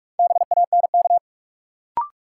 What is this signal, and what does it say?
Morse code, heard as a keyed tone, about 44 wpm, 700 Hz. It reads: BANK
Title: Sentences from Top 900 Words - New Word Drill - 45wpm